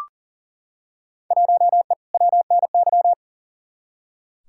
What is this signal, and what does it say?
Morse code, heard as a keyed tone, about 38 wpm, 700 Hz. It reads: 1E WNY
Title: Field Day Exchanges - 40wpm